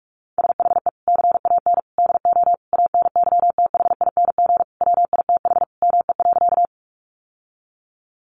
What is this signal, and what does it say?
Morse code, heard as a keyed tone, about 44 wpm, 700 Hz. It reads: SHE CAN DO ANYTHING WITH ME.
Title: Sentences from Top 1000 Words 45wpm